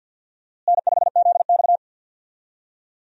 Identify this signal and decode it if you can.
Morse code, heard as a keyed tone, about 52 wpm, 700 Hz. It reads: N5ZX
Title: USA Call Signs - 4 Characters - Repeated 3x - 50wpm